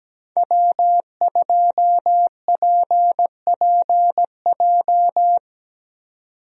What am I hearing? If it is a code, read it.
Morse code, heard as a keyed tone, about 17 wpm, 700 Hz. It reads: W2PPJ
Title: USA Call Signs - 5 Characters - 17wpm